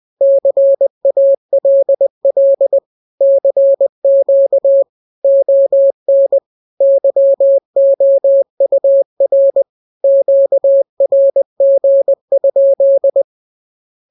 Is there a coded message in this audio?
Yes: Morse code from 0.2 to 13.2 s, heard as a keyed tone, about 20 wpm, 550 Hz. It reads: CALL CQ ON YOUR QRG?